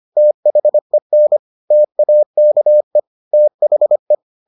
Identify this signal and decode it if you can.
Morse code, heard as a keyed tone, about 25 wpm, 600 Hz. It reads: THEN TAKE THE